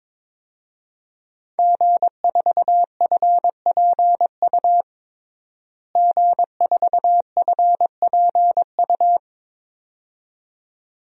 Morse code, heard as a keyed tone, about 22 wpm, 700 Hz. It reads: G4FPU G4FPU
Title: DXCC-100 Call Signs - 5 Characters - Repeated 2x - 22wpm